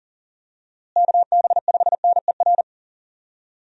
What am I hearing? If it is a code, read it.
Morse code, heard as a keyed tone, about 40 wpm, 700 Hz. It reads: KB5NER
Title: USA Call Signs - 6 Characters - Repeated 2x - 40wpm